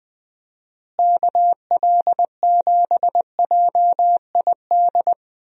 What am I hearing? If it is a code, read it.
Morse code, heard as a keyed tone, about 21 wpm, 700 Hz. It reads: KL7JID
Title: DXCC-100 Call Signs - 6 Characters - Repeated 2x - 20wpm